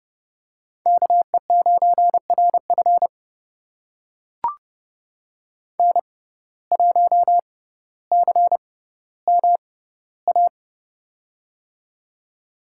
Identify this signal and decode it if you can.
Morse code, heard as a keyed tone, about 32 wpm, spaced as for 16 wpm, 700 Hz. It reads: KE9RF N1CMA
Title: USA Call Signs - 6x Character Spacing - 30wpm